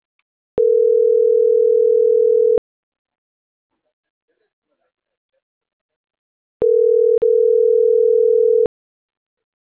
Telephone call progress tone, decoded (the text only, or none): ringback tone